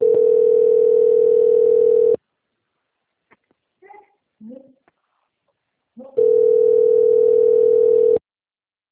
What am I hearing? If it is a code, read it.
Ringback tone (North America), 440 + 480 Hz.
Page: Valinhos - SP